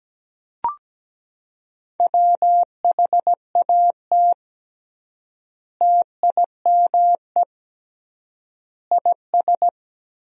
Morse code, heard as a keyed tone, about 17 wpm, 700 Hz. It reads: WHAT TIME IS
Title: 3x Word-Spacing - Sentences from Top 100 Words - 17wpm